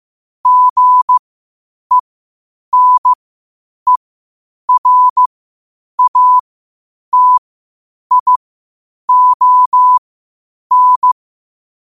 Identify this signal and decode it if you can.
Morse code, heard as a keyed tone, about 15 wpm, spaced as for 8 wpm, 1000 Hz. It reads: GENERATION
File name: qotdfiles / 8wpm